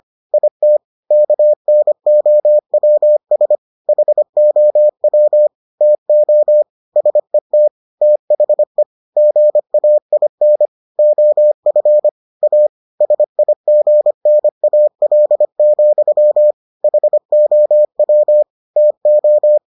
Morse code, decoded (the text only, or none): IT KNOWS HOW TO SET THE GAIN OF A SIGNAL, HOW TO